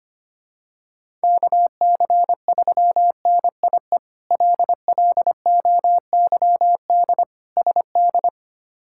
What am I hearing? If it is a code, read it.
Morse code, heard as a keyed tone, about 25 wpm, 700 Hz. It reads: KC3NIE LLOYD SD
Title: K1USN SST QSO Practice - 25wpm